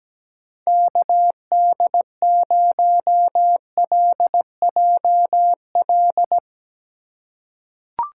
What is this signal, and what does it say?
Morse code, heard as a keyed tone, about 17 wpm, 700 Hz. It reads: KD0LJL
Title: USA Call Signs - 6 Characters - 4x Character Spacing - 17wpm